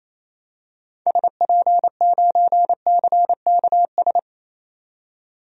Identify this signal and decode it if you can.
Morse code, heard as a keyed tone, about 28 wpm, 700 Hz. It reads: SP9CKS
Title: DXCC-100 Call Signs - 6 Characters - 28wpm